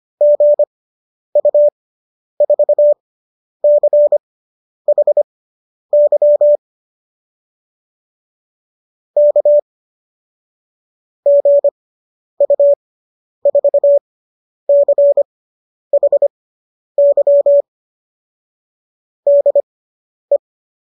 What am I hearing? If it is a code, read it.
Morse code, heard as a keyed tone, about 25 wpm, spaced as for 10 wpm, 570 Hz. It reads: GU4CHY K GU4CHY DE